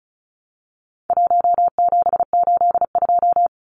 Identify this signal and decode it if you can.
Morse code, heard as a keyed tone, about 34 wpm, 700 Hz. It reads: 1782
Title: Numbers - 4-Digit 35wpm